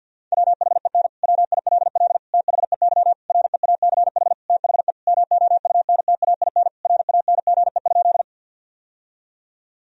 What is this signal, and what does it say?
Morse code, heard as a keyed tone, about 50 wpm, 700 Hz. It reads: WHEN WILL THEY REACH THE MOUNTAIN RANGE?
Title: Sentences from Top 1000 Words - Review Part 4 - 50wpm